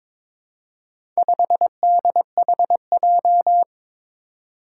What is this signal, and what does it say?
Morse code, heard as a keyed tone, about 22 wpm, 700 Hz. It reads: 5DHJ
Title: Four Challenging Letter-Number Combinations 22wpm